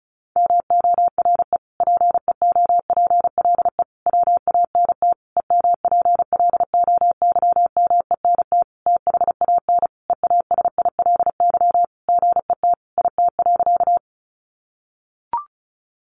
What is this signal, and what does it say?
Morse code, heard as a keyed tone, about 34 wpm, 700 Hz. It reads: MORE PEOPLE WANT EMPLOYMENT THAN EASILY GET IT.
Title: Sentences from Top 1500 Words 35wpm